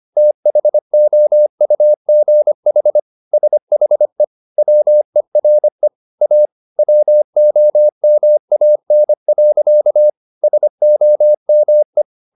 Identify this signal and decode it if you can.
Morse code, heard as a keyed tone, about 25 wpm, 600 Hz. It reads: THOUGH SHE WERE A WOMAN. SOME
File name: the old man and the sea part 1 25wpm